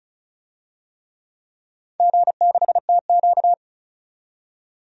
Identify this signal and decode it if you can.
Morse code, heard as a keyed tone, about 34 wpm, 700 Hz. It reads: G6TQ
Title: DXCC-100 Call Signs - 4 Characters - 35wpm